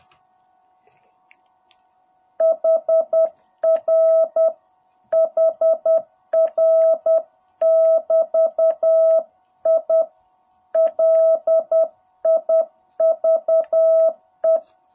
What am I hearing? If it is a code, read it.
Morse code, heard as a keyed tone, about 10 wpm, 660 Hz. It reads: HR HR <BT> I LIVE
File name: jh8cbh / 2021cw